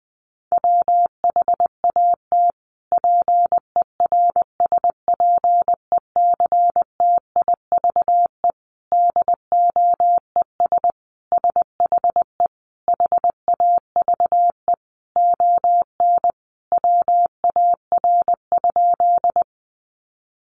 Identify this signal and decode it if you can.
Morse code, heard as a keyed tone, about 20 wpm, 700 Hz. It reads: WHAT PERSPECTIVE DOES SHE HAVE ON WAR?
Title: Sentences from Top 1200 Words 20wpm